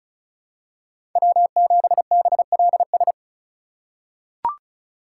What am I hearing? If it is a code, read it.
Morse code, heard as a keyed tone, about 34 wpm, 700 Hz. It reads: W7BLS
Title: USA Call Signs - 5 Characters - 4x Character Spacing - 35wpm